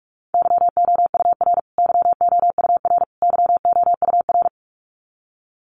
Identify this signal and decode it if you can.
Morse code, heard as a keyed tone, about 44 wpm, 700 Hz. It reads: YOUR YOUR YOUR
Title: Top 500 Words Repeated 45wpm